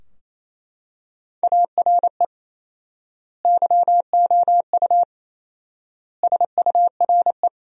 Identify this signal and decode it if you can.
Morse code, heard as a keyed tone, about 28 wpm, 700 Hz. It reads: ARE YOU SURE
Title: 4x Word-Spacing - Sentences from Top 300 Words - 28wpm